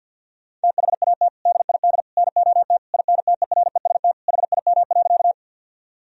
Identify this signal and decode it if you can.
Morse code, heard as a keyed tone, about 52 wpm, 700 Hz. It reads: THAT DID NOT INTEREST HIM.
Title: Sentences from Top 1000 Words - Review Part 2 - 50wpm